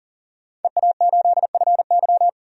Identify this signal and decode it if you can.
Morse code, heard as a keyed tone, about 38 wpm, 700 Hz. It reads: EA8FY
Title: DXCC-100 Call Signs - 5 Characters - 40wpm